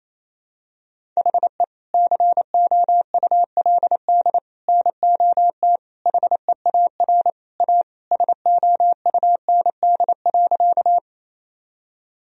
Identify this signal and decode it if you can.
Morse code, heard as a keyed tone, about 28 wpm, 700 Hz. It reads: HE COULD NOT HEAR A SOUND.
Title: Sentences from Top 1000 Words - Review Part 1 - 28wpm